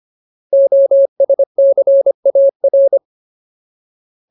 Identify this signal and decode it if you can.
Morse code, heard as a keyed tone, about 25 wpm, 550 Hz. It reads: OSCAR
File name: Ultimate Word List / 25WPM Ultimate